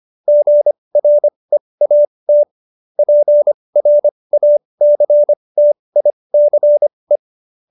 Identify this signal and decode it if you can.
Morse code, heard as a keyed tone, about 25 wpm, spaced as for 20 wpm, 590 Hz. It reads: GREAT PRACTICE
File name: QSO316 20